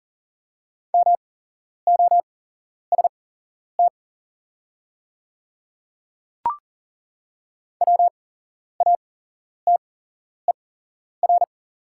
Morse code, heard as a keyed tone, about 40 wpm, spaced as for 11 wpm, 690 Hz. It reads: MOST WATER
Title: 8x Character Spacing - Top 100 Words - 40wpm